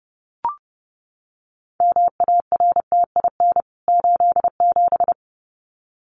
Morse code, heard as a keyed tone, about 30 wpm, 700 Hz. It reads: MARTIN 87
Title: CWT Exchange 30wpm